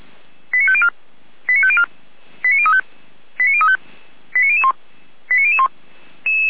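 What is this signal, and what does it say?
Five-tone selective call (ZVEI), heard as quick runs of five tones, tones of about 70 ms. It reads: 89504 89504 89035 89035 89001 89001
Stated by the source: ZVEI five-tone selective call, address 89010, received on SDR by feuerwehr-orlamuende (fire brigade Orlamünde).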